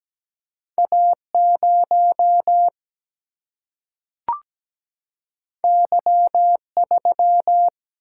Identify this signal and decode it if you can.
Morse code, heard as a keyed tone, about 17 wpm, 700 Hz. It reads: A 0 Y 3
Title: Single-Letter Single-Number 17wpm